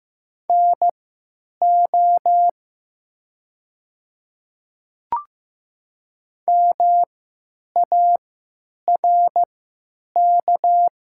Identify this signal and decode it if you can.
Morse code, heard as a keyed tone, about 15 wpm, spaced as for 9 wpm, 700 Hz. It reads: NO MARK
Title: 3x Character Spacing - Top 300 Words - 15wpm